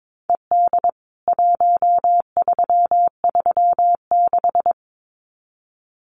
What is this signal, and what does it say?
Morse code, heard as a keyed tone, about 22 wpm, 700 Hz. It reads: ED 1336
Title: CWT Exchange 22wpm